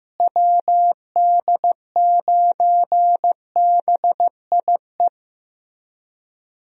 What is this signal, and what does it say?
Morse code, heard as a keyed tone, about 15 wpm, 700 Hz. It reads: WD9BIE
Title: USA Call Signs - 6 Characters - Repeated 3x - 15wpm